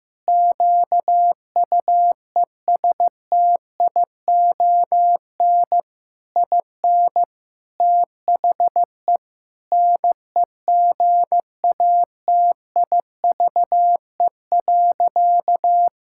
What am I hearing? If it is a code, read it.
Morse code, heard as a keyed tone, about 15 wpm, 700 Hz. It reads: QUESTION IN THE NEGATIVE.
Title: Sentences from Top 1000 Words - Review Part 6 - 15wpm